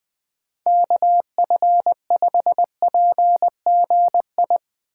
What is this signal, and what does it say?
Morse code, heard as a keyed tone, about 21 wpm, 700 Hz. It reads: KF5PGI
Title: USA Call Signs - 3x Character Spacing - 20wpm